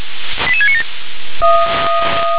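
Five-tone selective call (ZVEI), heard as quick runs of five tones, tones of about 80 ms. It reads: 88697
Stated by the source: Received on an SDR; ZVEI five-tone call 88693.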